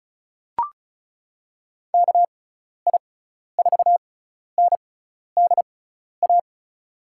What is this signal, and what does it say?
Morse code, heard as a keyed tone, about 35 wpm, spaced as for 12 wpm, 690 Hz. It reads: KI4NDA
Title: USA Call Signs - 6 Characters - 6x Character Spacing - 35wpm